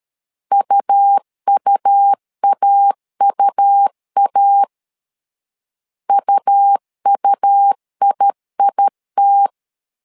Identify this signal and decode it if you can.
Morse code, heard as a keyed tone, about 13 wpm, 800 Hz. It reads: UUAUA UUIIT
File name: LEZIONI / 2 UHDN / 2 U 12wpm 1338